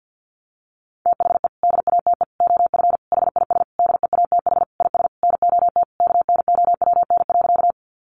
Morse code, heard as a keyed tone, about 48 wpm, 700 Hz. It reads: THE DATE OF HIS DEATH IS NOT KNOWN.
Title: Sentences from Top 1100 Words 50wpm